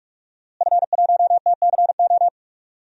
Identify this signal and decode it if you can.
Morse code, heard as a keyed tone, about 41 wpm, 700 Hz. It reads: F1TCO
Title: DXCC-100 Call Signs - 5 Characters - Repeated 3x - 45wpm